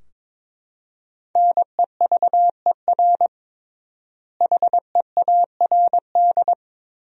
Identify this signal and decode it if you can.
Morse code, heard as a keyed tone, about 21 wpm, 700 Hz. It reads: NEVER HEARD
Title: 3x Word-Spacing - Sentences from Top 400 Words - 22wpm